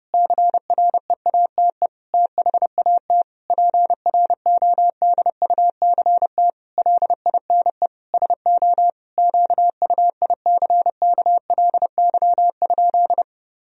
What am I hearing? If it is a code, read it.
Morse code, heard as a keyed tone, about 30 wpm, 700 Hz. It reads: CREATE THAT PRODUCT LINE SO QUICKLY?
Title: Sentences from Top 1000 Words - Review Part 3 - 30wpm